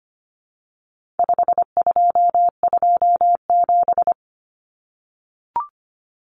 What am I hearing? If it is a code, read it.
Morse code, heard as a keyed tone, about 25 wpm, 700 Hz. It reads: 5227
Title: Common Cut Numbers - 4-Digit 25wpm